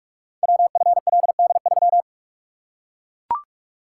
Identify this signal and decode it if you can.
Morse code, heard as a keyed tone, about 44 wpm, 700 Hz. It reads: WFLD3
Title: Code Groups - 5 Letter:Numbers - 45wpm